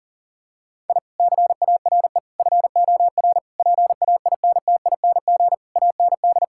Morse code, heard as a keyed tone, about 40 wpm, 700 Hz. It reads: I CARE FOR PAINTING AND
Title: Sentences from Top 1000 Words - Review Part 6 - 40wpm